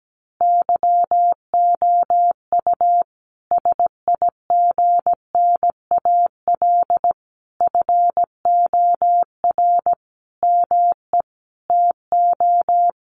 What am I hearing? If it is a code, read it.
Morse code, heard as a keyed tone, about 17 wpm, 700 Hz. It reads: YOU SIGNAL FOR ME TO
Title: Sentences from Top 1400 Words 17wpm